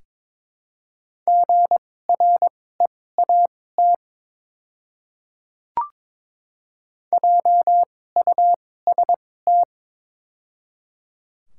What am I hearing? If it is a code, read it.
Morse code, heard as a keyed tone, about 22 wpm, spaced as for 16 wpm, 700 Hz. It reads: GREAT JUST
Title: 2x Character Spacing - Top 300 Words - 22wpm